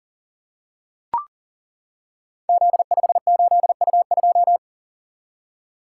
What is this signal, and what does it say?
Morse code, heard as a keyed tone, about 41 wpm, 700 Hz. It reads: Z58U2
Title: Code Groups - 5 Letter:Numbers - 40wpm